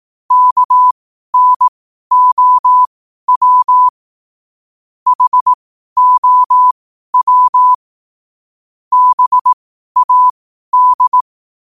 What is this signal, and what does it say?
Morse code, heard as a keyed tone, about 18 wpm, spaced as for 12 wpm, 1000 Hz. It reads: KNOW HOW BAD
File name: qotdfiles / 12wpm